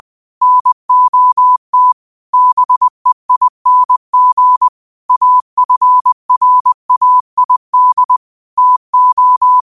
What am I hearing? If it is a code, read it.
Morse code, heard as a keyed tone, about 20 wpm, 1000 Hz. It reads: NOT BEING AFRAID TO